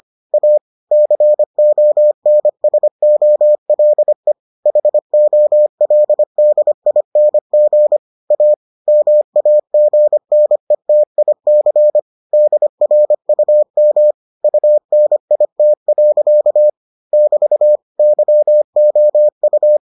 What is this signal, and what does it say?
Morse code, heard as a keyed tone, about 25 wpm, 600 Hz. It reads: A CONSOLE HOLDING A MAGNETIC DRUM UNIT. <BT> YOU